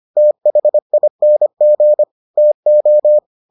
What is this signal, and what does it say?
Morse code, heard as a keyed tone, about 25 wpm, 600 Hz. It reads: THING TO